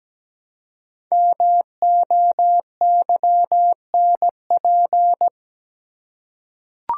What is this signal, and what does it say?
Morse code, heard as a keyed tone, about 17 wpm, 700 Hz. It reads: MOYNP
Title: Code Groups - 5 Letter:Numbers - 17wpm